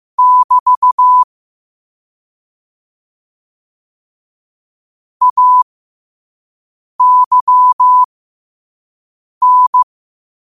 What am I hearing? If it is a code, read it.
Morse code, heard as a keyed tone, about 15 wpm, spaced as for 5 wpm, 1000 Hz. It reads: <BT> AYN
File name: qotdfiles / 5wpm